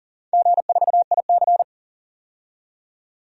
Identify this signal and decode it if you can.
Morse code, heard as a keyed tone, about 40 wpm, 700 Hz. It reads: G4IC
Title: DXCC-100 Call Signs - 4 Characters - 40wpm